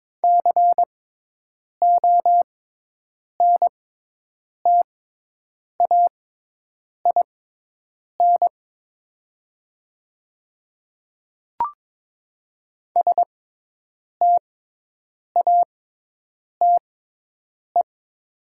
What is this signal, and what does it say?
Morse code, heard as a keyed tone, about 22 wpm, spaced as for 8 wpm, 700 Hz. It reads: CONTAIN STATE
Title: 6x Character Spacing - Top 500 Words - 22wpm